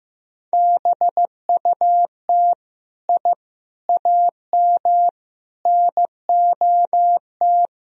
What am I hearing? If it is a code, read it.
Morse code, heard as a keyed tone, about 15 wpm, 700 Hz. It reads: BUT I AM NOT
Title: Sentences from Top 1000 Words - Review Part 2 - 15wpm